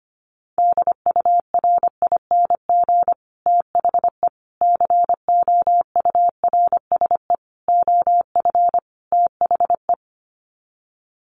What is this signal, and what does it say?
Morse code, heard as a keyed tone, about 25 wpm, 700 Hz. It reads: DURING THE COURSE OF THE
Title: Sets of 5 Words Encore 25wpm